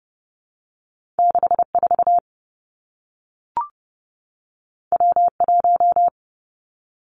Morse code, heard as a keyed tone, about 30 wpm, 700 Hz. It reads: 6 4 W 1
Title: Two Challenging Letter-Number Combinations 30wpm